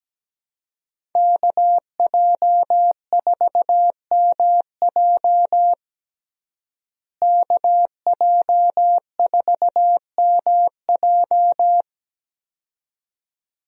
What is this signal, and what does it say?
Morse code, heard as a keyed tone, about 17 wpm, 700 Hz. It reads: KJ4MJ KJ4MJ
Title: USA Call Signs - 5 Characters - Repeated 2x - 17wpm